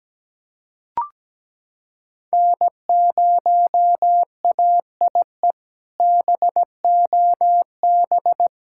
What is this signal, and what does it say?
Morse code, heard as a keyed tone, about 17 wpm, 700 Hz. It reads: N0AIE BOB
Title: K1USN SST QSO Practice - 17wpm